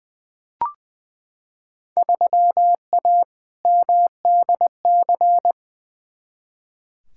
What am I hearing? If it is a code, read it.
Morse code, heard as a keyed tone, about 20 wpm, 700 Hz. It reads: 3A MDC
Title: Field Day Exchanges - 20wpm